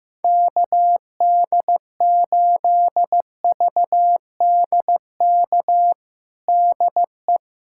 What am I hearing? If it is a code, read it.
Morse code, heard as a keyed tone, about 15 wpm, 700 Hz. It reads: KD8VDK DE
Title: USA Callsign DE Callsign 15wpm